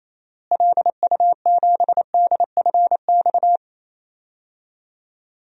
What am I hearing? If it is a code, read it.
Morse code, heard as a keyed tone, about 27 wpm, 700 Hz. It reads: LU7DFX
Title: DXCC-100 Call Signs - 6 Characters - Repeated 2x - 28wpm